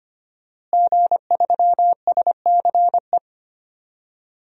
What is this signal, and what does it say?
Morse code, heard as a keyed tone, about 25 wpm, 700 Hz. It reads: G3SCE
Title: DXCC-100 Call Signs - 5 Characters - Repeated 2x - 25wpm